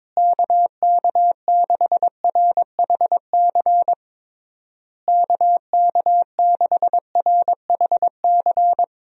Morse code, heard as a keyed tone, about 22 wpm, 700 Hz. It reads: KK6RHC KK6RHC
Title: USA Call Signs - 6 Characters - Repeated 3x - 22wpm